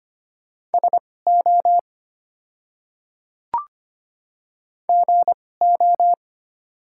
Morse code, heard as a keyed tone, about 25 wpm, spaced as for 18 wpm, 700 Hz. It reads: SO GO
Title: 2x Character Spacing - Top 100 Words - 25wpm